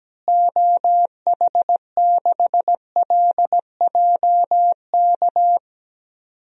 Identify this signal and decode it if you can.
Morse code, heard as a keyed tone, about 17 wpm, 700 Hz. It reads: OH6LJK
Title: DXCC-100 Call Signs - 6 Characters - Repeated 2x - 17wpm